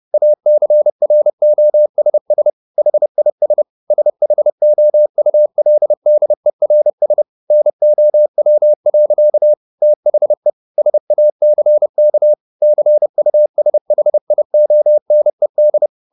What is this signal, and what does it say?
Morse code, heard as a keyed tone, about 30 wpm, 600 Hz. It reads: ACROSS HIS SHOULDERS NOW. THE SACK CUSHIONED
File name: the old man and the sea / the old man and the sea 1 30wpm